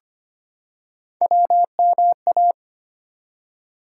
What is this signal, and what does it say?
Morse code, heard as a keyed tone, about 25 wpm, 700 Hz. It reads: WMA
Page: morse-code-ninja-practice-25wpm